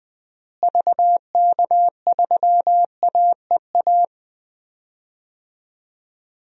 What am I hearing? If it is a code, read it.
Morse code, heard as a keyed tone, about 20 wpm, 700 Hz. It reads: VK3AEA
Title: DXCC-100 Call Signs - 6 Characters - Repeated 2x - 20wpm